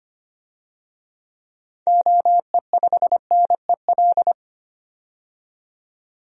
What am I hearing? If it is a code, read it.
Morse code, heard as a keyed tone, about 25 wpm, 700 Hz. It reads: OE5NEL
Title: DXCC-100 Call Signs - 6 Characters - 25wpm